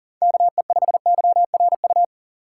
Morse code, heard as a keyed tone, about 40 wpm, 700 Hz. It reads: KE5YRU
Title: USA Call Signs - 6 Characters - 6x Character Spacing - 40wpm